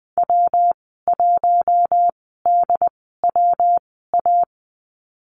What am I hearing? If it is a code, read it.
Morse code, heard as a keyed tone, about 20 wpm, spaced as for 14 wpm, 700 Hz. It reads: W1DWA
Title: 193 Lesson 31g - Letter v - Call Signs 20wpm